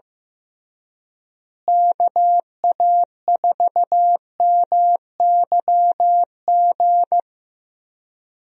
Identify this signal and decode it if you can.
Morse code, heard as a keyed tone, about 15 wpm, 700 Hz. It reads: KA4MYG
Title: USA Call Signs - 6 Characters - Repeated 3x - 15wpm